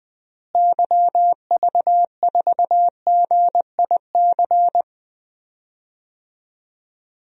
Morse code, heard as a keyed tone, about 20 wpm, 700 Hz. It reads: YV4GIC
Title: DXCC-100 Call Signs - 6 Characters - Repeated 3x - 20wpm